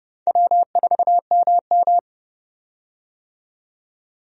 Morse code, heard as a keyed tone, about 31 wpm, 700 Hz. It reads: W4MM
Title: USA Call Signs - 4 Characters - 30wpm